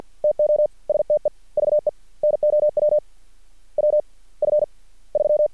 Morse code, heard as a keyed tone, about 50 wpm, spaced as for 31 wpm, 600 Hz. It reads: TO STEVE NOW WF3
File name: archived / CRSnet